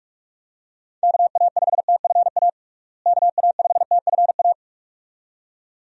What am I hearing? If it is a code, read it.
Morse code, heard as a keyed tone, about 46 wpm, 700 Hz. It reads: KA5TFA KA5TFA
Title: USA Call Signs - 6 Characters - Repeated 2x - 45wpm